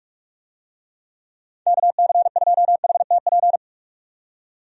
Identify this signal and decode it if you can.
Morse code, heard as a keyed tone, about 46 wpm, 700 Hz. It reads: KC2HTP
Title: USA Call Signs - 6 Characters - 45wpm